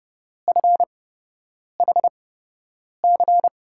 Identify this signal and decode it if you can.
Morse code, heard as a keyed tone, about 30 wpm, spaced as for 8 wpm, 700 Hz. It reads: FHC
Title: USA Call Signs - 8x Character Spacing - 30wpm